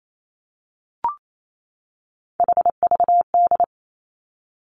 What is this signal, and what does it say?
Morse code, heard as a keyed tone, about 28 wpm, 700 Hz. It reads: HVD
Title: Three Challenging Letter-Number Combinations 28wpm